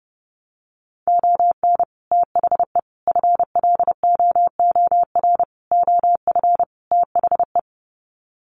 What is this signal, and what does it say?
Morse code, heard as a keyed tone, about 30 wpm, 700 Hz. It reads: ON THE FLOOR OF THE
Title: Sets of 5 Words 30wpm